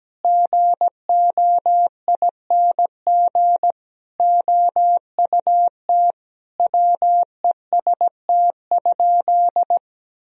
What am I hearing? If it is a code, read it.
Morse code, heard as a keyed tone, about 17 wpm, 700 Hz. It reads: GOING OUT WEST?
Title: Sentences from Top 1000 Words - Review Part 2 - 17wpm